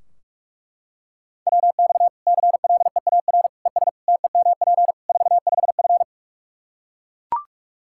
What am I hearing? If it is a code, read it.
Morse code, heard as a keyed tone, about 41 wpm, 700 Hz. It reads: WX CLEAR ES TEMP 45F
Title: Practice QSO - Exchange 2 - Info - 45wpm